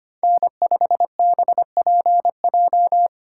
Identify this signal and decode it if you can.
Morse code, heard as a keyed tone, about 25 wpm, 700 Hz. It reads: N5BPJ
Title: USA Call Signs - 4x Character Spacing - 25wpm